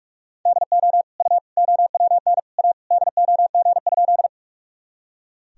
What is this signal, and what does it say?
Morse code, heard as a keyed tone, about 46 wpm, 700 Hz. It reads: DO U OWN A DOG?
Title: Sentences from Top 400 Words and Abbreviations 45wpm